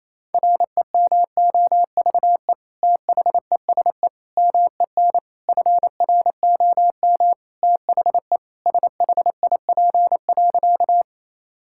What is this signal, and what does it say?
Morse code, heard as a keyed tone, about 28 wpm, 700 Hz. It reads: REMOVE THESE MEN FROM THE SHIP.
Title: Sentences from Top 1000 Words - Review Part 5 - 28wpm